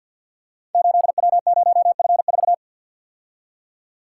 Morse code, heard as a keyed tone, about 51 wpm, 700 Hz. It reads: 8W0F4
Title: Code Groups - 5 Letter:Numbers - 50wpm